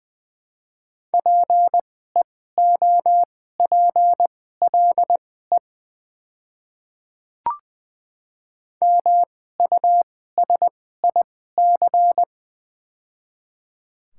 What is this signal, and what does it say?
Morse code, heard as a keyed tone, about 21 wpm, spaced as for 15 wpm, 700 Hz. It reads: PEOPLE MUSIC
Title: 2x Character Spacing - Top 300 Words - 20wpm